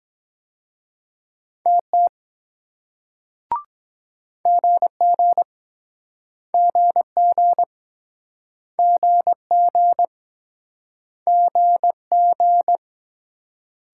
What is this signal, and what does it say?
Morse code, heard as a keyed tone, about 23 wpm, 700 Hz. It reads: TT GG GG GG GG